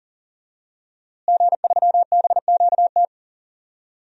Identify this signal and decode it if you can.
Morse code, heard as a keyed tone, about 39 wpm, 700 Hz. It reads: G3BQT
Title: DXCC-100 Call Signs - 5 Characters - Repeated 3x - 40wpm